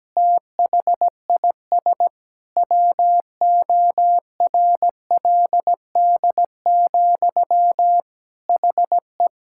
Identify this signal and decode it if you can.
Morse code, heard as a keyed tone, about 17 wpm, 700 Hz. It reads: THIS WORLD, HE
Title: Sentences from Top 1600 Words 17wpm